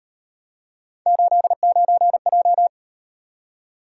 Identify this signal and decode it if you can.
Morse code, heard as a keyed tone, about 37 wpm, 700 Hz. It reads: 89J